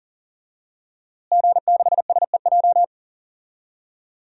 Morse code, heard as a keyed tone, about 40 wpm, 700 Hz. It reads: G6SEJ
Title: DXCC-100 Call Signs - 5 Characters - Repeated 2x - 40wpm